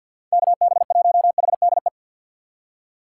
Morse code, heard as a keyed tone, about 46 wpm, 700 Hz. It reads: KB1HDE
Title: USA Call Signs - 6 Characters - 3x Character Spacing - 50wpm